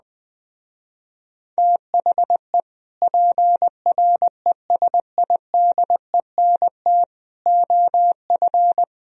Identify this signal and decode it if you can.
Morse code, heard as a keyed tone, about 20 wpm, 700 Hz. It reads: THE PRESIDENT OF
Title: Speed-Racing - Sets of 4 Words 17wpm